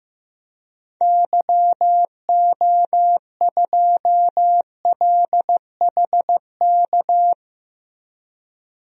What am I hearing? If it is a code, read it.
Morse code, heard as a keyed tone, about 15 wpm, 700 Hz. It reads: YO2LHK
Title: DXCC-100 Call Signs - 6 Characters - 15wpm